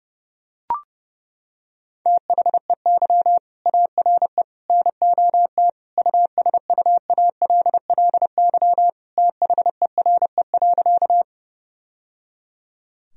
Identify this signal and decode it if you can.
Morse code, heard as a keyed tone, about 30 wpm, 700 Hz. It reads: THEY ARE NOT USUALLY THERE.
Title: Sentences from Top 1000 Words - Review Part 3 - 30wpm